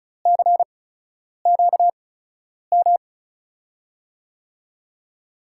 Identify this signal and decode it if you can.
Morse code, heard as a keyed tone, about 34 wpm, spaced as for 10 wpm, 700 Hz. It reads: CQM
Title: USA Call Signs - 6 Characters - 8x Character Spacing - 35wpm